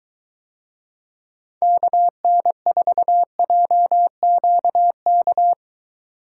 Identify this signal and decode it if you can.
Morse code, heard as a keyed tone, about 23 wpm, 700 Hz. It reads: KN4JQK